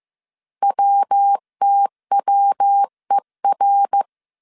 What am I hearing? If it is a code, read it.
Morse code, heard as a keyed tone, about 15 wpm, 800 Hz. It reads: WTWER